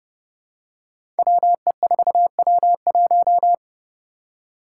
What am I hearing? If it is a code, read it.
Morse code, heard as a keyed tone, about 30 wpm, 700 Hz. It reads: WE4W1
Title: Five Challenging Letter-Number Combinations 30wpm